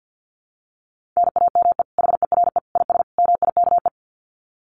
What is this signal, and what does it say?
Morse code, heard as a keyed tone, about 50 wpm, 700 Hz. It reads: NAME HERE IS MIKE
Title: Common QSO Phrases 50wpm